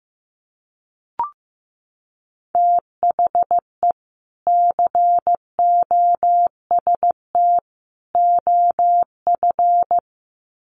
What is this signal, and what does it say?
Morse code, heard as a keyed tone, about 15 wpm, 700 Hz. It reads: THE COST OF
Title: Sets of 3 Words 15wpm